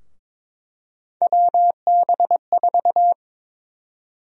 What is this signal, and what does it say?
Morse code, heard as a keyed tone, about 22 wpm, 700 Hz. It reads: WB4
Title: USA Call Sign Prefixes - 22wpm